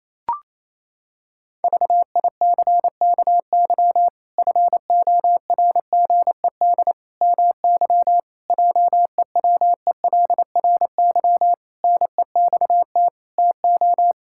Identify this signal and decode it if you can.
Morse code, heard as a keyed tone, about 28 wpm, 700 Hz. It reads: VICKY FORGED MY JEWELRY NEXT TO
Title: Pangram Challenge v2 - 28wpm